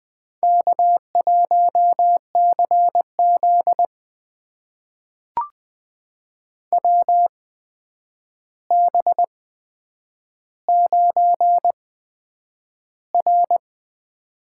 Morse code, heard as a keyed tone, about 20 wpm, spaced as for 9 wpm, 700 Hz. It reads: K1CZ WB9R
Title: USA Call Signs - 4 Characters - 8x Character Spacing - 20wpm